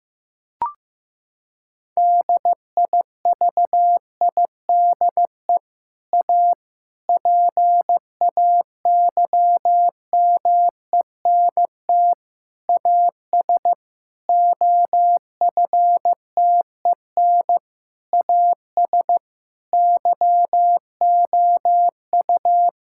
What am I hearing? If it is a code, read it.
Morse code, heard as a keyed tone, about 15 wpm, 700 Hz. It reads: DIVIDE A PAYMENT AS OFTEN AS YOU
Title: Sentences from Top 1600 Words 15wpm